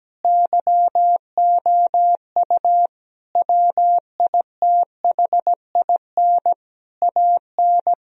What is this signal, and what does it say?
Morse code, heard as a keyed tone, about 17 wpm, 700 Hz. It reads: YOU WITHIN AN
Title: Sentences from Top 1000 Words - Review Part 4 - 17wpm